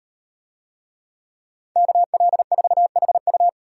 Morse code, heard as a keyed tone, about 38 wpm, 700 Hz. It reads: KL4HU